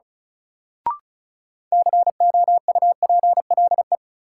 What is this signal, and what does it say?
Morse code, heard as a keyed tone, about 36 wpm, 700 Hz. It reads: COUPLE
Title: Sentences from Top 1100 Words - New Word Drill - 35wpm